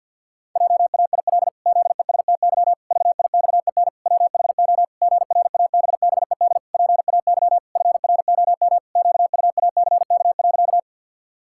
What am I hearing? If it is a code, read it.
Morse code, heard as a keyed tone, about 53 wpm, 700 Hz. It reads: JAIL ZESTY VIXEN WHO GRABBED PAY FROM QUACK.
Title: Pangram Challenge v2 - 50wpm